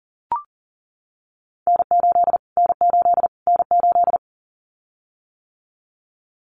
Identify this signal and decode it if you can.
Morse code, heard as a keyed tone, about 40 wpm, 700 Hz. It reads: N8 N8 N8
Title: Callsign Prefixes for Top 25 DXCC - Repeated 3x - 40wpm